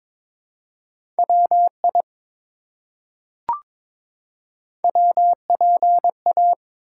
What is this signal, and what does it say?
Morse code, heard as a keyed tone, about 22 wpm, 700 Hz. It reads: WI WPA
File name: ARRL Sections v2 - 22wpm